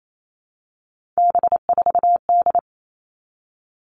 Morse code, heard as a keyed tone, about 28 wpm, 700 Hz. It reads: B4D
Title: All Cut Numbers - 3-Digit 28wpm